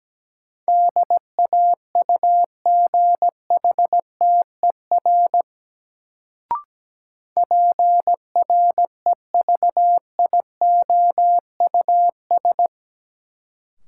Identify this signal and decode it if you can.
Morse code, heard as a keyed tone, about 17 wpm, 700 Hz. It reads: DAUGHTER PREVIOUS
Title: Sentences from Top 1500 Words - New Word Drill - 17wpm